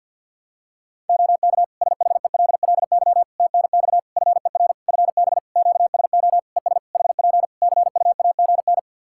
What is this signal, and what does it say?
Morse code, heard as a keyed tone, about 52 wpm, 700 Hz. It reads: OK SHELLY TNX FER FB QSO ES HP CUAGN
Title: Practice QSO - Exchange 4 - Ending - 50wpm